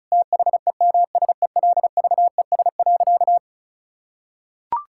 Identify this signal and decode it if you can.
Morse code, heard as a keyed tone, about 34 wpm, 700 Hz. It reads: THEMSELVES.
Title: Sentences from Top 1000 Words - Review Part 5 - 35wpm